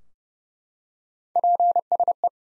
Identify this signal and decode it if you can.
Morse code, heard as a keyed tone, about 30 wpm, 700 Hz. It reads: PSE
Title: Top 90 Abbr 30WPM